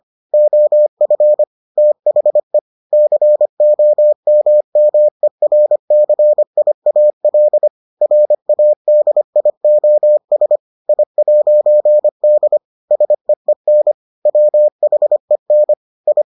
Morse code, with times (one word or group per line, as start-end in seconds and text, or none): 0.3-1.4 OF
1.8-2.6 THE
2.9-7.7 COMMERCIAL
8.0-10.6 RADIOS
10.9-12.6 I'D
12.9-13.9 SEEN
14.3-15.7 WHEN
16.1-16.2 I